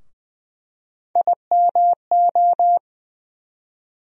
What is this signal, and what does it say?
Morse code, heard as a keyed tone, about 21 wpm, 700 Hz. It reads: IMO
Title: USA Call Sign Postfixes - 3 Letters 20wpm